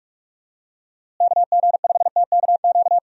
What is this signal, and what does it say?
Morse code, heard as a keyed tone, about 48 wpm, 700 Hz. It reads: KG5TKQ
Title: USA Call Signs - 6 Characters - Repeated 2x - 45wpm